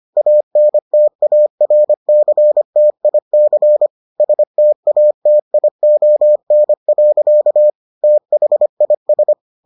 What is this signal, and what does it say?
Morse code, heard as a keyed tone, about 25 wpm, 600 Hz. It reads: ANTARCTIC STATION. THIS